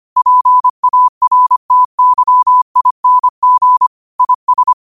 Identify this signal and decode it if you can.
Morse code, heard as a keyed tone, about 25 wpm, 1000 Hz. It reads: PARTYING IS
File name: qotdfiles / 25wpm